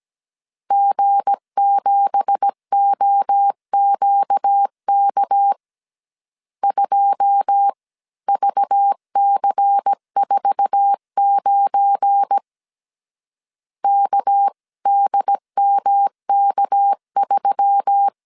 Morse code, heard as a keyed tone, about 17 wpm, 800 Hz. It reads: G7OQK 2VC49 KDMK3
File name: CW / DET / 1 9 1500C S C 16wpm 2909